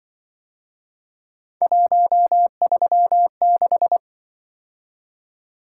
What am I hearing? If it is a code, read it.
Morse code, heard as a keyed tone, about 23 wpm, 700 Hz. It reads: 136
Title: Speed-Racing - Three Letter-Number Combinations - 20wpm